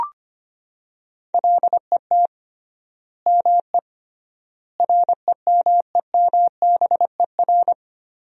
Morse code, heard as a keyed tone, about 25 wpm, 700 Hz. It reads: LET ME REMEMBER
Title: 3x Word-Spacing - Sentences from Top 400 Words - 25wpm